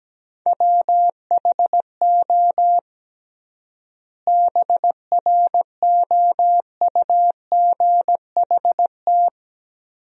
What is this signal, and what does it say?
Morse code, heard as a keyed tone, about 17 wpm, 700 Hz. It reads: WHO BROUGHT
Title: 3x Word-Spacing - Sentences from Top 500 Words - 17wpm